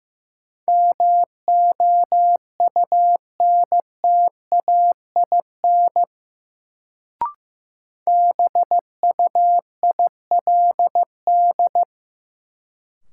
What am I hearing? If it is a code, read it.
Morse code, heard as a keyed tone, about 15 wpm, 700 Hz. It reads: MOUNTAIN BUILD
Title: Sentences from Top 1000 Words - Review Part 1 - New Word Drill - 15wpm